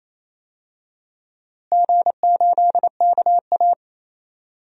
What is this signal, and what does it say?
Morse code, heard as a keyed tone, about 27 wpm, 700 Hz. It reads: G8KA
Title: DXCC-100 Call Signs - 4 Characters - Repeated 3x - 28wpm